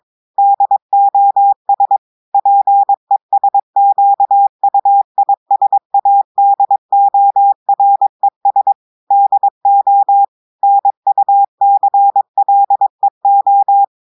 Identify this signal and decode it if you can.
Morse code, heard as a keyed tone, about 22 wpm, 800 Hz. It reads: DOS PESQUISADORES DO NUCLEO
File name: audios / portugais / portugais texte en clair 1 vitesse 22 mots minutes